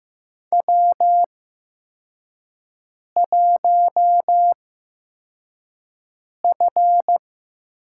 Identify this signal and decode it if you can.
Morse code, heard as a keyed tone, about 15 wpm, spaced as for 4 wpm, 700 Hz. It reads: W1F
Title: USA Call Signs - 4 Characters - 8x Character Spacing - 15wpm